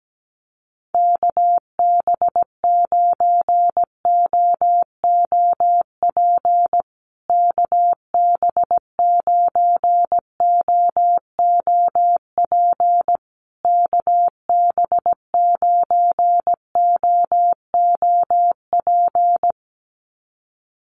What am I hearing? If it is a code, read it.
Morse code, heard as a keyed tone, about 17 wpm, 700 Hz. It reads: KB9OOP KB9OOP KB9OOP
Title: USA Call Signs Repeated 3x 17wpm